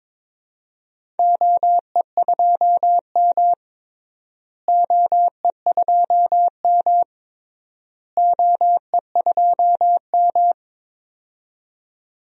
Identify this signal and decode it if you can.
Morse code, heard as a keyed tone, about 21 wpm, 700 Hz. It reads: OE2M OE2M OE2M
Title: DXCC-100 Call Signs - 4 Characters - Repeated 3x - 22wpm